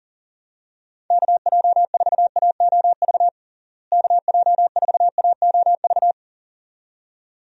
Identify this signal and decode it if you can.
Morse code, heard as a keyed tone, about 40 wpm, 700 Hz. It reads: KJ4AOV KJ4AOV
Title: USA Call Signs - 6 Characters - Repeated 2x - 40wpm